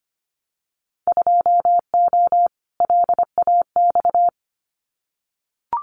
Morse code, heard as a keyed tone, about 25 wpm, 700 Hz. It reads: 2O LAX
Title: Winter Field Day Exchanges 25wpm